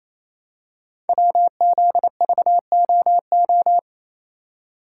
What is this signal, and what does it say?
Morse code, heard as a keyed tone, about 27 wpm, 700 Hz. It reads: WZVOO
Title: Code Groups - 5 Letter:Numbers - 28wpm